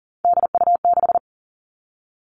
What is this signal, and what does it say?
Morse code, heard as a keyed tone, about 40 wpm, 700 Hz. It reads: DU6
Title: Three Challenging Letter-Number Combinations 40wpm